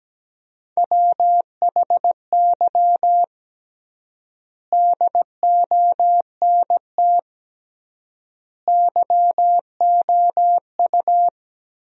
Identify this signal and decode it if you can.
Morse code, heard as a keyed tone, about 17 wpm, 700 Hz. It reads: WHY DONT YOU
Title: 3x Word-Spacing - Sentences from Top 300 Words - 17wpm